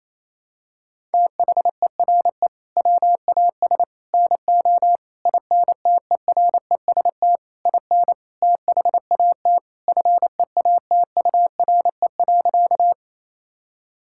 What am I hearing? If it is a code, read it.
Morse code, heard as a keyed tone, about 28 wpm, 700 Hz. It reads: THERE WAS NO INTEREST IN THAT FEATURE.
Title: Sentences from Top 1000 Words - Review Part 5 - 28wpm